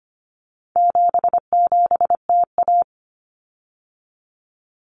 Morse code, heard as a keyed tone, about 25 wpm, 700 Hz. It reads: 77TA
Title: Common Cut Numbers - 4-Digit 25wpm